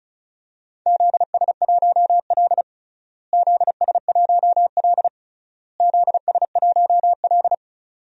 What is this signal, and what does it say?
Morse code, heard as a keyed tone, about 34 wpm, 700 Hz. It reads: ZS1L ZS1L ZS1L
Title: DXCC-100 Call Signs - 4 Characters - Repeated 3x - 35wpm